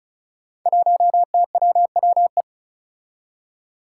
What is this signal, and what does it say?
Morse code, heard as a keyed tone, about 36 wpm, 700 Hz. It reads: 1TWWE
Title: Five Challenging Letter-Number Combinations 35wpm